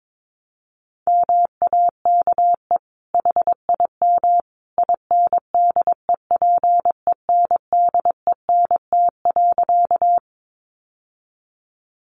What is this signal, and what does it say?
Morse code, heard as a keyed tone, about 22 wpm, 700 Hz. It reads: MAKE HIM INDEPENDENT.
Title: Sentences from Top 900 Words 22wpm